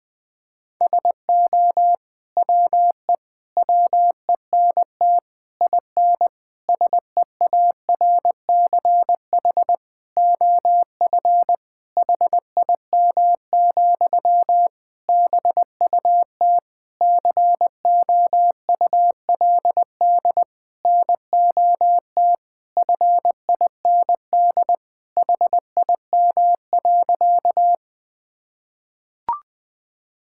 Morse code, heard as a keyed tone, about 20 wpm, 700 Hz. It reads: SO WE WENT IN SEARCH OF HIM, BUT COULD NOT FIND HIM.
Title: Sentences from Top 1000 Words - Review Part 5 - 20wpm